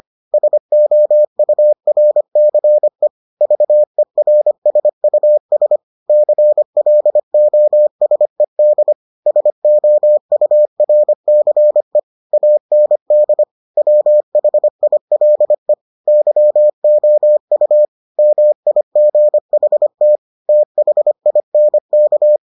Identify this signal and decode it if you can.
Morse code, heard as a keyed tone, about 25 wpm, 600 Hz. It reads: SOURCE VERSUS CLOSED SOURCE AND WHILE YOU MIGHT THINK